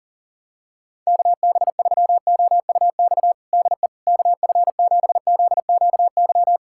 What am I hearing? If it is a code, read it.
Morse code, heard as a keyed tone, about 40 wpm, 700 Hz. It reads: KB3OUX DE KF7ZQY
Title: USA Callsign DE Callsign - 6 Characters - 40wpm